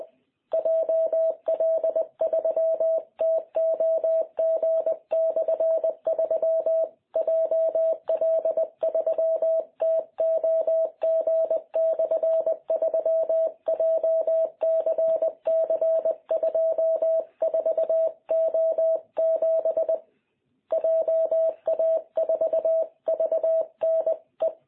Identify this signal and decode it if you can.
Morse code, heard as a keyed tone, about 20 wpm, 650 Hz. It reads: JL3TOG/3 JL3TOG/3 JCC24O7 JA4VNE
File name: jh8cbh / 2021cw